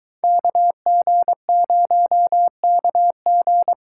Morse code, heard as a keyed tone, about 23 wpm, 700 Hz. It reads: KG0KG